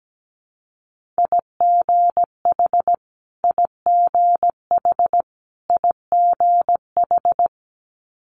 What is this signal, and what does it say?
Morse code, heard as a keyed tone, about 17 wpm, 700 Hz. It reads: IGH IGH IGH
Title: Multi-letter Phonemes Repeated 3x 17wpm